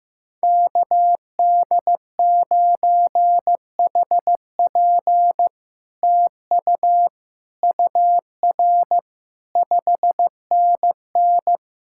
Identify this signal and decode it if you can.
Morse code, heard as a keyed tone, about 15 wpm, 700 Hz. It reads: KD9HP TU UR 5NN
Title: SOTA QSO Practice 15wpm